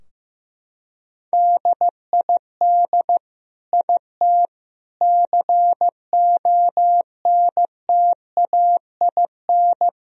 Morse code, heard as a keyed tone, about 15 wpm, 700 Hz. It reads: DID IT CONTAIN
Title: Sentences from Top 1000 Words - Review Part 2 - 15wpm